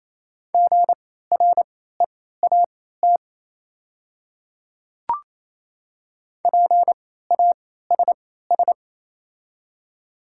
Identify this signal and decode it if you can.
Morse code, heard as a keyed tone, about 28 wpm, spaced as for 16 wpm, 700 Hz. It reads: GREAT PASS
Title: 3x Character Spacing - Top 500 Words - 28wpm